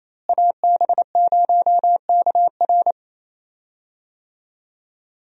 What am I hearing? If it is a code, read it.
Morse code, heard as a keyed tone, about 28 wpm, 700 Hz. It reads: AB0KR
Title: USA Call Signs - 5 Characters - Repeated 2x - 28wpm